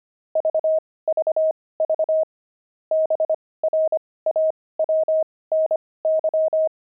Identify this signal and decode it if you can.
Morse code, heard as a keyed tone, about 25 wpm, spaced as for 18 wpm, 630 Hz. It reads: VVV BRAWNY
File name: pangram-fs-18-cs-25-brawny